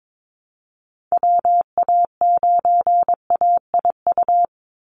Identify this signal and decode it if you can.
Morse code, heard as a keyed tone, about 22 wpm, 700 Hz. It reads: WA9AIU
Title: USA Call Signs 22wpm